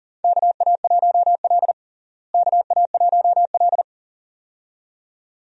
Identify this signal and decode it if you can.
Morse code, heard as a keyed tone, about 40 wpm, 700 Hz. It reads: KA1L KA1L
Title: USA Call Signs - 4 Characters - Repeated 2x - 40wpm